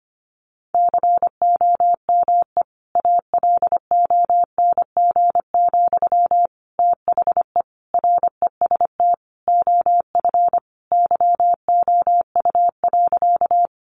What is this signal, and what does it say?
Morse code, heard as a keyed tone, about 25 wpm, 700 Hz. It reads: COME ALONG, THE REST OF YOU.
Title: Sentences from Top 700 Words 25wpm